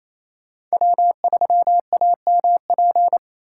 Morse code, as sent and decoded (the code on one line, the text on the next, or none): .-- ...-- .- -- .--.
W3AMP